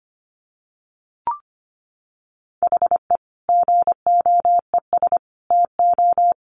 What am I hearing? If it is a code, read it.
Morse code, heard as a keyed tone, about 25 wpm, 700 Hz. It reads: HE GOES TO
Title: Additional Easy Phrases 25wpm